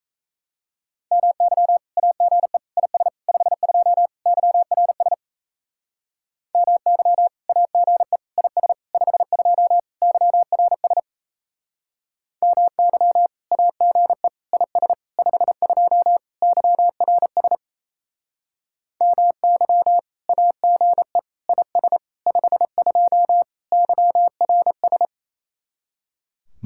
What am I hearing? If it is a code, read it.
Morse code, heard as a keyed tone, about 35 wpm, 700 Hz. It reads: MY AGE IS 52 YRS MY AGE IS 52 YRS MY AGE IS 52 YRS MY AGE IS 52 YRS
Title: Speed-Racing Common QSO Phrases Encore - 28wpm